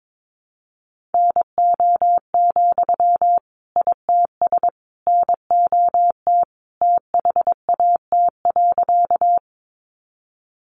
Morse code, heard as a keyed tone, about 22 wpm, 700 Hz. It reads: NO, ITS NOT THAT.
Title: Sentences from Top 600 Words 22wpm